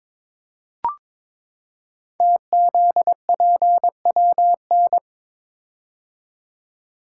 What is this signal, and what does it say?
Morse code, heard as a keyed tone, about 23 wpm, 700 Hz. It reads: TZPWN
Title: Code Groups - 5 Letter:Numbers - 22wpm